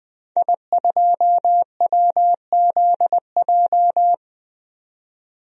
Morse code, heard as a keyed tone, about 20 wpm, 700 Hz. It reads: I2WZJ
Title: DXCC-100 Call Signs - 5 Characters - 20wpm